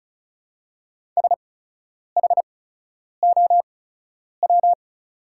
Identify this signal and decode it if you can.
Morse code, heard as a keyed tone, about 36 wpm, spaced as for 10 wpm, 700 Hz. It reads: SHOW
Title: 8x Character Spacing - Top 300 Words - 35wpm